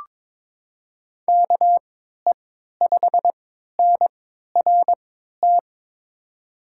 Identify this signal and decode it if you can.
Morse code, heard as a keyed tone, about 21 wpm, spaced as for 12 wpm, 700 Hz. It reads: KE5NRT
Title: USA Call Signs - 6 Characters - 3x Character Spacing - 22wpm